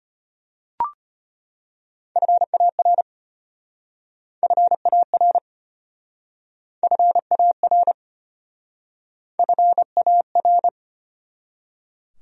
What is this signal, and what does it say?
Morse code, heard as a keyed tone, about 33 wpm, 700 Hz. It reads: FAR FAR FAR FAR